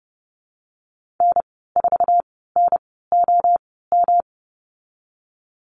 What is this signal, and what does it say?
Morse code, heard as a keyed tone, about 30 wpm, spaced as for 17 wpm, 700 Hz. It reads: N4NOM
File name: Lesson 19g - Letter w - Call Signs 30wpm